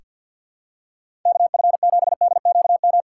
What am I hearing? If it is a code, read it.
Morse code, heard as a keyed tone, about 48 wpm, 700 Hz. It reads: KF7DQM
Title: USA Call Signs - 6 Characters - 6x Character Spacing - 50wpm